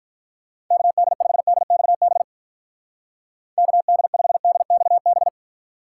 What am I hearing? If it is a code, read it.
Morse code, heard as a keyed tone, about 44 wpm, 700 Hz. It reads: KD5DXB KD5DXB